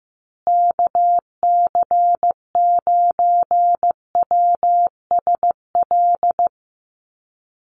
Morse code, heard as a keyed tone, about 15 wpm, 700 Hz. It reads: KC9WSL
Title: USA Call Signs Repeated 2x 15wpm